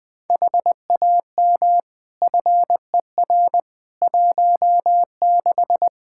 Morse code, heard as a keyed tone, about 20 wpm, 700 Hz. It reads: HAM FER 16
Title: K1USN SST QSO Practice 20wpm